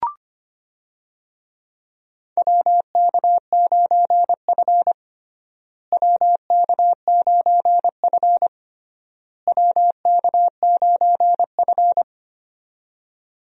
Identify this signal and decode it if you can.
Morse code, heard as a keyed tone, about 25 wpm, 700 Hz. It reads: WK9F WK9F WK9F
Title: USA Call Signs - 4 Characters - Repeated 3x - 25wpm